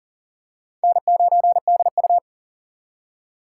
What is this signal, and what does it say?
Morse code, heard as a keyed tone, about 37 wpm, 700 Hz. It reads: N9DU
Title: USA Call Signs - 4 Characters - 3x Character Spacing - 40wpm